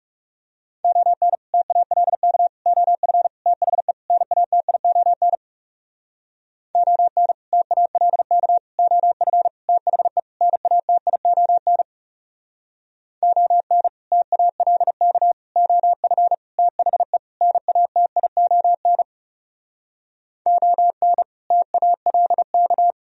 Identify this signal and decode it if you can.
Morse code, heard as a keyed tone, about 39 wpm, 700 Hz. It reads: ON TALK OF THE NATION ON TALK OF THE NATION ON TALK OF THE NATION ON TALK